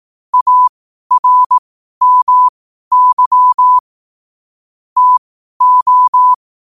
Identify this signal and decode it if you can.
Morse code, heard as a keyed tone, about 18 wpm, spaced as for 12 wpm, 1000 Hz. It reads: ARMY TO
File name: qotdfiles / 12wpm